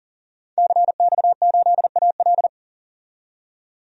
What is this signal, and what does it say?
Morse code, heard as a keyed tone, about 37 wpm, 700 Hz. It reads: CX8AL
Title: DXCC-100 Call Signs - 5 Characters - 40wpm